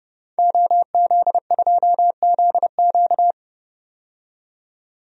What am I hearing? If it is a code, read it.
Morse code, heard as a keyed tone, about 30 wpm, 700 Hz. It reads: OZ2ZQ
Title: DXCC-100 Call Signs - 5 Characters - 30wpm